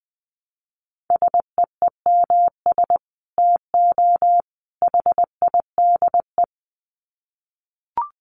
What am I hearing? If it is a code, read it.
Morse code, heard as a keyed tone, about 20 wpm, 700 Hz. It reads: SEEMS TO HIDE
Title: Dit Challenge 20wpm